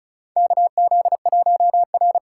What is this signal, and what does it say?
Morse code, heard as a keyed tone, about 36 wpm, 700 Hz. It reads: KZ1R
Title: USA Call Signs - 4 Characters - 2x Character Spacing - 35wpm